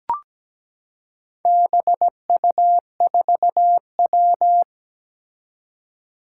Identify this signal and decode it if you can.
Morse code, heard as a keyed tone, about 17 wpm, 700 Hz. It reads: BU4W
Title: Four Challenging Letter-Number Combinations 17wpm